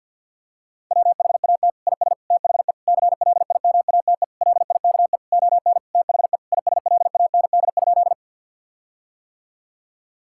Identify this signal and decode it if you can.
Morse code, heard as a keyed tone, about 50 wpm, 700 Hz. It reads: WHAT IS THE CLIMATE LIKE ON THE ISLAND?
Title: Sentences from Top 1600 Words 50wpm